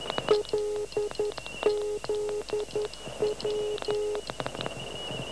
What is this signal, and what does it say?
Morse code, heard as a keyed tone, about 11 wpm, 420 Hz, with noise. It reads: LZW
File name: LZW 12880 1846